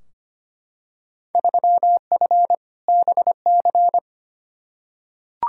Morse code, heard as a keyed tone, about 25 wpm, 700 Hz. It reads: 3F BC
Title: Field Day Exchanges Encore - 25wpm